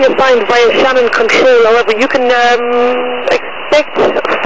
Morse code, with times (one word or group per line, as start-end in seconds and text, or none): none